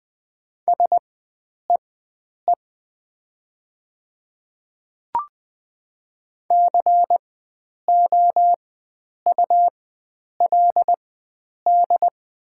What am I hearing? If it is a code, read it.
Morse code, heard as a keyed tone, about 20 wpm, spaced as for 9 wpm, 700 Hz. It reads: SEE COULD
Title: 4x Character Spacing - Top 100 Words - 20wpm